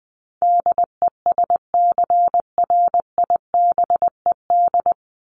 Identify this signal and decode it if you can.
Morse code, heard as a keyed tone, about 20 wpm, 700 Hz. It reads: DESCRIBED
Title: ED Suffix - Top 100 Words with Base Words 20wpm